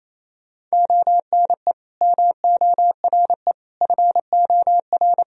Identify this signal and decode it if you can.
Morse code, heard as a keyed tone, about 28 wpm, 700 Hz. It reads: ONE MORE FOR
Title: Sentences from Top 1000 Words - Review Part 2 - 28wpm